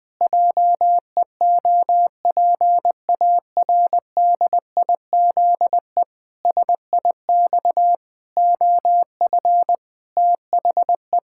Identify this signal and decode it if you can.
Morse code, heard as a keyed tone, about 20 wpm, 700 Hz. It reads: JEOPARDIZE SIX OF THE
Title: Pangram Challenge v2 - 20wpm